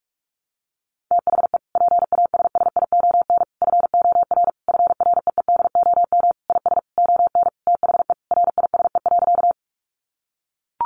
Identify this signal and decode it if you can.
Morse code, heard as a keyed tone, about 44 wpm, 700 Hz. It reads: THE PASSION FOR FREEDOM IS ON THE RISE.
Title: Sentences from Top 1300 Words 45wpm